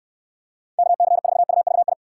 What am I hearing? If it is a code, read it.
Morse code, heard as a keyed tone, about 69 wpm, 700 Hz. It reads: KC4UVI